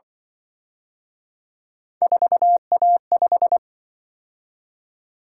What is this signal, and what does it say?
Morse code, heard as a keyed tone, about 25 wpm, 700 Hz. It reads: 4A5
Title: Speed-Racing - Three Letter-Number Combinations - 20wpm